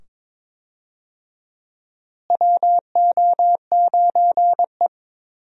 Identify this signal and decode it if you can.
Morse code, heard as a keyed tone, about 22 wpm, 700 Hz. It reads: WO9E
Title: USA Call Signs - 4 Characters - Repeated 2x - 22wpm